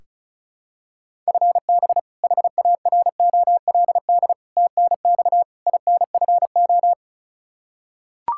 Morse code, heard as a keyed tone, about 36 wpm, 700 Hz. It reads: FB HAROLD TNX INFO
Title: Practice QSO - Exchange 3 - More Info - 35wpm